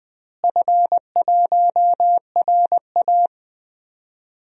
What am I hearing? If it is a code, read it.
Morse code, heard as a keyed tone, about 20 wpm, 700 Hz. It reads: F1RA
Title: DXCC-100 Call Signs - 4 Characters - Repeated 2x - 20wpm